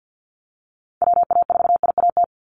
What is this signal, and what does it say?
Morse code, heard as a keyed tone, about 50 wpm, 700 Hz. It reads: WA4IAT
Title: USA Call Signs 50wpm